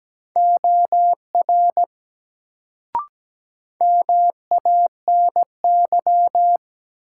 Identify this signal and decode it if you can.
Morse code, heard as a keyed tone, about 17 wpm, 700 Hz. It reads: OR MANY
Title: Sets of 2 Words - New Word Drill - 17wpm